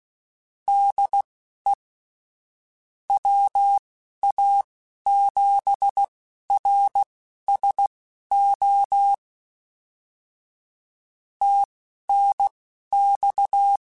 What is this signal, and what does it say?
Morse code, heard as a keyed tone, about 16 wpm, spaced as for 11 wpm, 800 Hz. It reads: DE WA7RSO TNX
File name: QSOs / QSOs-mp3s / kc7haa@10